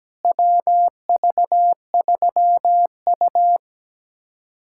Morse code, heard as a keyed tone, about 17 wpm, 700 Hz. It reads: WV3U
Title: USA Callsign DE Callsign - 4 Characters - 17wpm